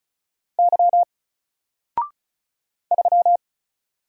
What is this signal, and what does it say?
Morse code, heard as a keyed tone, about 34 wpm, 700 Hz. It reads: Y 3
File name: Single-Letter-Number-35wpm